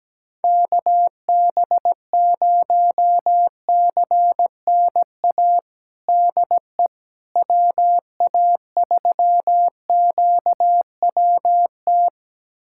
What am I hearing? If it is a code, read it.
Morse code, heard as a keyed tone, about 17 wpm, 700 Hz. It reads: KB0CNA DE WA3QWT
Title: USA Callsign DE Callsign - 6 Characters - 17wpm